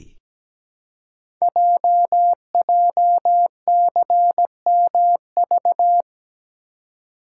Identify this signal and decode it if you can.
Morse code, heard as a keyed tone, about 17 wpm, 700 Hz. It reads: JJCMV
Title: Code Groups - 5 Letters - 17wpm